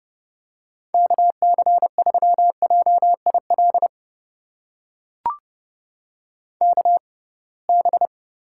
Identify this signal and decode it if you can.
Morse code, heard as a keyed tone, about 30 wpm, spaced as for 23 wpm, 700 Hz. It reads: KC3JIL KB
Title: USA Call Signs - 6 Characters - 6x Character Spacing - 30wpm